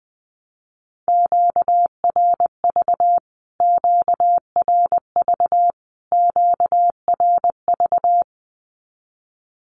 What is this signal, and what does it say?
Morse code, heard as a keyed tone, about 20 wpm, 700 Hz. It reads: QRV QRV QRV
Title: Q-Codes Repeated 3x 20wpm